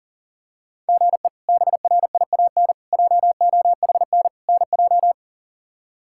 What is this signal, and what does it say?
Morse code, heard as a keyed tone, about 40 wpm, 700 Hz. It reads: GE BRIAN JOHN NJ
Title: K1USN SST QSO Practice - 40wpm